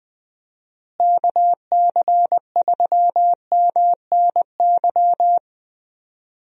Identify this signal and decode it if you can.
Morse code, heard as a keyed tone, about 21 wpm, 700 Hz. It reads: KC3MNY
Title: USA Call Signs - 6 Characters - Repeated 2x - 20wpm